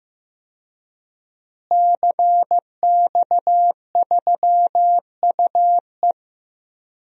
Morse code, heard as a keyed tone, about 15 wpm, 700 Hz. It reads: CX3UE
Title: DXCC-100 Call Signs - 5 Characters - Repeated 3x - 15wpm